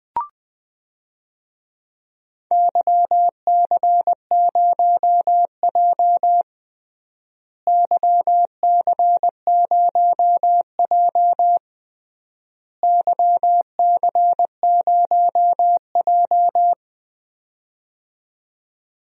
Morse code, heard as a keyed tone, about 20 wpm, 700 Hz. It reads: YC0J YC0J YC0J
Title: DXCC-100 Call Signs - 4 Characters - Repeated 3x - 20wpm